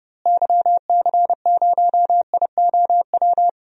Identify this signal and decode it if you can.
Morse code, heard as a keyed tone, about 31 wpm, 700 Hz. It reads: YC0IOW
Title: DXCC-100 Call Signs - 6 Characters - Repeated 3x - 30wpm